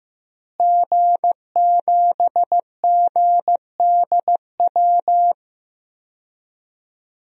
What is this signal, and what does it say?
Morse code, heard as a keyed tone, about 15 wpm, 700 Hz. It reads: G7GDW
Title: DXCC-100 Call Signs - 5 Characters - Repeated 2x - 15wpm